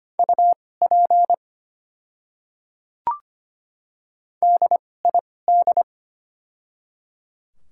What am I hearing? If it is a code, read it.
Morse code, heard as a keyed tone, about 25 wpm, spaced as for 18 wpm, 700 Hz. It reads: UP DID
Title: 2x Character Spacing - Top 100 Words - 25wpm